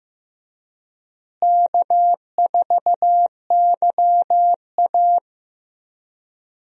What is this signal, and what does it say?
Morse code, heard as a keyed tone, about 16 wpm, 700 Hz. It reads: K4YA
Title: USA Call Signs - 4 Characters - Repeated 2x - 15wpm